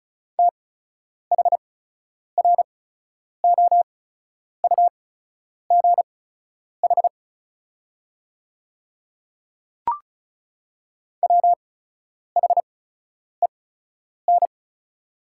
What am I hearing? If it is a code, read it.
Morse code, heard as a keyed tone, about 36 wpm, spaced as for 10 wpm, 700 Hz. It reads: THROUGH WHEN
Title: 8x Character Spacing - Top 200 Words - 35wpm